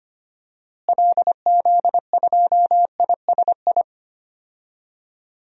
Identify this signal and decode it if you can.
Morse code, heard as a keyed tone, about 25 wpm, 700 Hz. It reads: LZ2ISI
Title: DXCC-100 Call Signs - 6 Characters - Repeated 3x - 25wpm